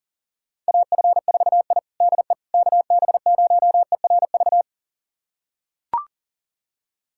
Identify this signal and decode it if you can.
Morse code, heard as a keyed tone, about 40 wpm, 700 Hz. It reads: AF4I DE KB0ERV
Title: USA Callsign DE Callsign 40wpm